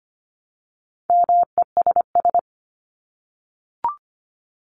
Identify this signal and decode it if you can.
Morse code, heard as a keyed tone, about 25 wpm, 700 Hz. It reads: MESS
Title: Top 1751-2000 Words 25wpm